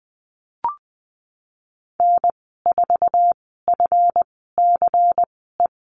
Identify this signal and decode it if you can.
Morse code, heard as a keyed tone, about 20 wpm, spaced as for 15 wpm, 700 Hz. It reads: N4FCE
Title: 154 Lesson 25g - Letter p - Call Signs 20wpm